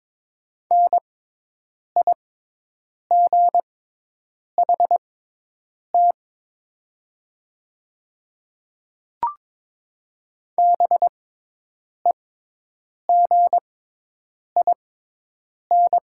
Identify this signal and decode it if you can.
Morse code, heard as a keyed tone, about 22 wpm, spaced as for 8 wpm, 700 Hz. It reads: NIGHT BEGIN
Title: 6x Character Spacing - Top 500 Words - 22wpm